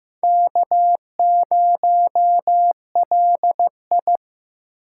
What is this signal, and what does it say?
Morse code, heard as a keyed tone, about 18 wpm, 700 Hz. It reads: K0LI
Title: Speed Racing - USA 1x2 Call Signs 15wpm